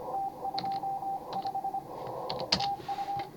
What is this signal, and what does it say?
Morse code, heard as a keyed tone, about 26 wpm, 810 Hz, with radio noise. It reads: T25 TU